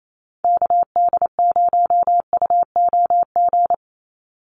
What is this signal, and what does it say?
Morse code, heard as a keyed tone, about 28 wpm, 700 Hz. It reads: KD0UOG
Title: USA Call Signs 28wpm